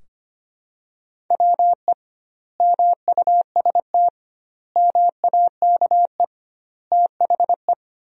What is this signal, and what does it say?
Morse code, heard as a keyed tone, about 25 wpm, 700 Hz. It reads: WE MUST MAKE THE
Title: 2x Word-Spacing - Sentences from Top 400 Words - 25wpm